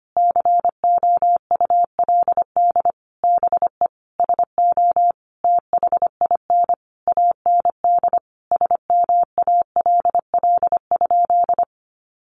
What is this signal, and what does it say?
Morse code, heard as a keyed tone, about 25 wpm, 700 Hz. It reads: COULD BE SO THIN AND SMALL?
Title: Sentences from Top 1200 Words 25wpm